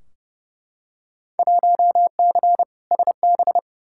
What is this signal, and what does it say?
Morse code, heard as a keyed tone, about 30 wpm, 700 Hz. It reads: 1C SB
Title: Field Day Exchanges Encore - 30wpm